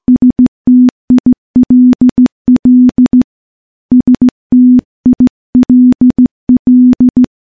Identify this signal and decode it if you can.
Morse code, heard as a keyed tone, about 15 wpm, 250 Hz. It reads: STILL STILL
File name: morse / morse-training-15WPM